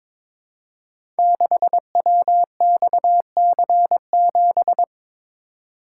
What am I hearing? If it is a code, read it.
Morse code, heard as a keyed tone, about 22 wpm, 700 Hz. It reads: 6WXC7
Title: Code Groups - 5 Letter:Numbers - 22wpm